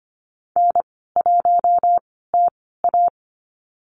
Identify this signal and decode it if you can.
Morse code, heard as a keyed tone, about 25 wpm, spaced as for 16 wpm, 700 Hz. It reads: N1TA
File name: Lesson 15g - Number 5 - Call Signs 25wpm